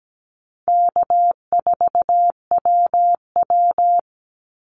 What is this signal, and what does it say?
Morse code, heard as a keyed tone, about 17 wpm, 700 Hz. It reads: K4WW
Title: Easy Call Signs - 17wpm